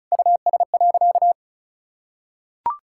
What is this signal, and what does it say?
Morse code, heard as a keyed tone, about 36 wpm, 700 Hz. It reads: US.
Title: Sentences from Top 1600 Words 35wpm